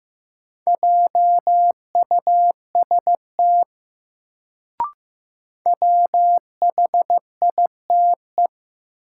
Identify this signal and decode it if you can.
Morse code, heard as a keyed tone, about 15 wpm, 700 Hz. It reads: JUST WHITE
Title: Sets of 2 Words - New Word Drill - 15wpm